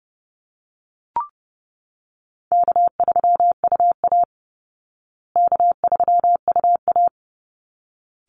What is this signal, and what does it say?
Morse code, heard as a keyed tone, about 28 wpm, 700 Hz. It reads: K3UA K3UA
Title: Easy Call Signs Repeated 2x - 30wpm